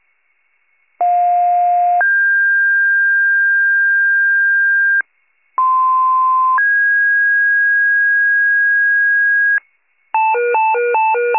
Two-tone page, heard as a one-second tone, then a three-second tone, about 710 then 1600 Hz, about 1040 then 1650 Hz. It is heard from 1.0 to 9.6 s.